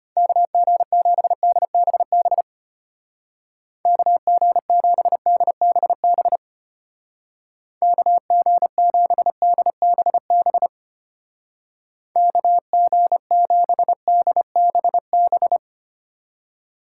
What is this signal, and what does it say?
Morse code, heard as a keyed tone, about 31 wpm, 700 Hz. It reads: KG7DBB KG7DBB KG7DBB KG7DBB
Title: Speed Racing - USA 2x3 Call Signs 25wpm